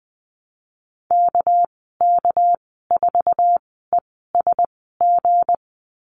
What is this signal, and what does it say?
Morse code, heard as a keyed tone, about 20 wpm, spaced as for 15 wpm, 700 Hz. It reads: KK4ESG
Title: 200 Lesson 32g - Letter k - Call Signs 20wpm